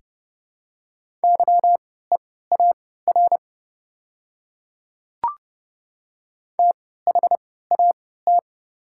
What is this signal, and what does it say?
Morse code, heard as a keyed tone, about 30 wpm, spaced as for 17 wpm, 700 Hz. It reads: YEAR THAT
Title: 3x Character Spacing - Top 200 Words - 30wpm